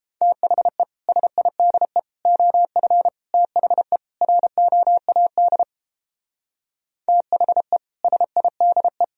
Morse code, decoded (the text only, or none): THE SIDE OF THE ROAD THE SIDE